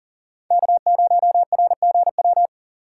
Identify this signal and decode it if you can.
Morse code, heard as a keyed tone, about 40 wpm, 700 Hz. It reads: K0RGW
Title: USA Call Signs - 5 Characters - 6x Character Spacing - 40wpm